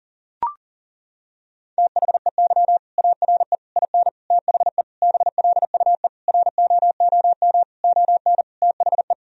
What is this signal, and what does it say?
Morse code, heard as a keyed tone, about 40 wpm, 700 Hz. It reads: THEY ARE IN THE BLUE ROOM ON THE
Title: Sentences from Top 1000 Words - Review Part 3 - 40wpm